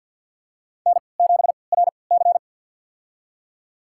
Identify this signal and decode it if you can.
Morse code, heard as a keyed tone, about 47 wpm, spaced as for 28 wpm, 700 Hz. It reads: N7RC
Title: USA Call Signs - 4 Characters - 3x Character Spacing - 50wpm